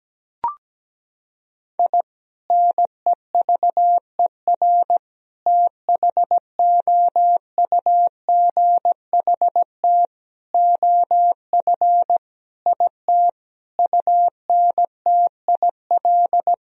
Morse code, heard as a keyed tone, about 17 wpm, 700 Hz. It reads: I NEVER THOUGHT OF IT UNTIL
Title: Sentences from Top 1000 Words - Review Part 1 - 17wpm